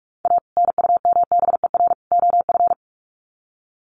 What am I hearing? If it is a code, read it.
Morse code, heard as a keyed tone, about 44 wpm, 700 Hz. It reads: A NUMBER OF
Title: Sets of 3 Words 45wpm